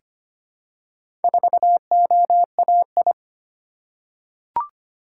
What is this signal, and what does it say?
Morse code, heard as a keyed tone, about 25 wpm, 690 Hz. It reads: 4OAI
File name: practice / 25wpm / Four Letter-Number Combinations - 25wpm